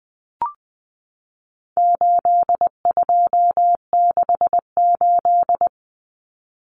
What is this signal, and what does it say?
Morse code, heard as a keyed tone, about 20 wpm, 700 Hz. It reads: 8268
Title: Numbers - 4-Digit 20wpm